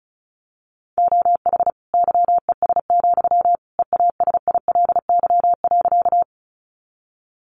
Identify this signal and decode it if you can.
Morse code, heard as a keyed tone, about 34 wpm, 700 Hz. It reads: OH YES, EASILY.
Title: Sentences from Top 900 Words 35wpm